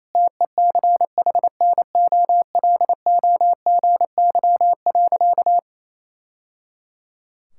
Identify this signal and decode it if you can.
Morse code, heard as a keyed tone, about 28 wpm, 700 Hz. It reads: TECHNOLOGY.
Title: Sentences from Top 1000 Words - Review Part 5 - 28wpm